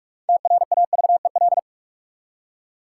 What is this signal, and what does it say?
Morse code, heard as a keyed tone, about 44 wpm, 700 Hz. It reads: TRAVEL
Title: Six Letter Words - 45wpm